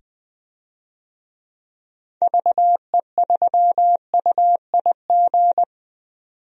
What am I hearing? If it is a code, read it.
Morse code, heard as a keyed tone, about 21 wpm, 700 Hz. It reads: VE3UIG
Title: DXCC-100 Call Signs - 6 Characters - 20wpm